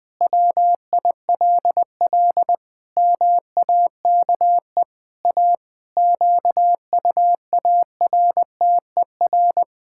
Morse code, heard as a keyed tone, about 20 wpm, 700 Hz. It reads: WILL MAKE A QUARTER
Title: Pangram Challenge v2 - 20wpm